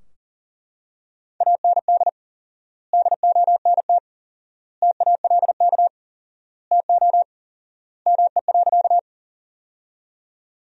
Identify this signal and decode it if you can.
Morse code, heard as a keyed tone, about 40 wpm, 700 Hz. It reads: AND DONT TALK TO ME.
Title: 4x Word-Spacing - Sentences from Top 400 Words - 40wpm